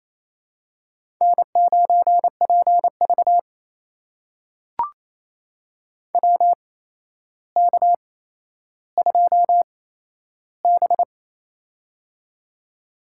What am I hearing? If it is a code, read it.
Morse code, heard as a keyed tone, about 27 wpm, spaced as for 12 wpm, 700 Hz. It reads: N9PV WK2B
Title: USA Call Signs - 4 Characters - 8x Character Spacing - 28wpm